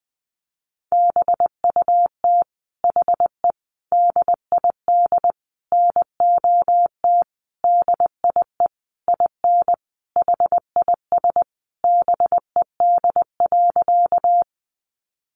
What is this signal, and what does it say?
Morse code, heard as a keyed tone, about 20 wpm, 700 Hz. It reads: BUT HE DID NOT DIE IN HIS BED.
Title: Sentences from Top 1300 Words 20wpm